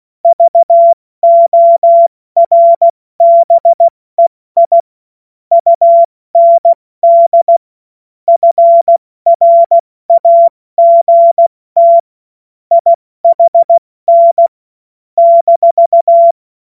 Morse code, heard as a keyed tone, about 16 wpm, spaced as for 14 wpm, 680 Hz. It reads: VORBEI UND FRAGT IHN -
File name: Morsekurs / Witze / Witz 17 Juni Mai 2025 70 F80